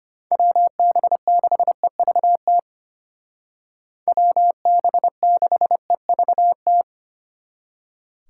Morse code, heard as a keyed tone, about 32 wpm, 700 Hz. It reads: WB6EVT WB6EVT